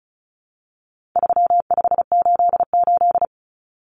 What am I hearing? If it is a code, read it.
Morse code, heard as a keyed tone, about 34 wpm, 700 Hz. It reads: 3588
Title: Common Cut Numbers - 4-Digit 35wpm